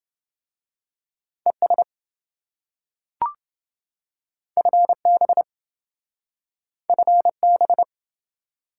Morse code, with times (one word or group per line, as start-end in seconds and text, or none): 1.5-1.8 ES
4.6-5.4 FB
6.9-7.8 FB